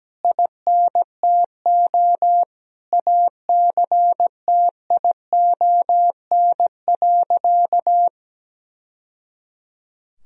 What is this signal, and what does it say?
Morse code, heard as a keyed tone, about 17 wpm, 700 Hz. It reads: INTO ACTION.
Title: Sentences from Top 1000 Words - Review Part 4 - 17wpm